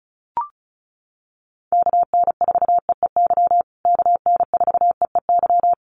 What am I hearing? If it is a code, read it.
Morse code, heard as a keyed tone, about 34 wpm, 700 Hz. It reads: KN4EEY KN4EEY
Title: USA Call Signs Repeated 3x 35wpm